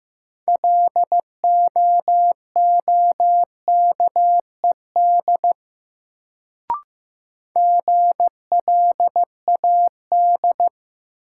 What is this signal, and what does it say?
Morse code, heard as a keyed tone, about 15 wpm, 700 Hz. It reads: LOOKED GLAD
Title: Sentences from Top 800 Words - New Word Drill - 15wpm